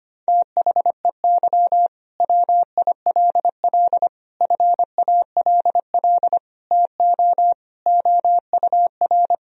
Morse code, heard as a keyed tone, about 25 wpm, 700 Hz. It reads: THEY WILL FALL TO OUR
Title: Sentences from Top 1000 Words - Review Part 4 - 25wpm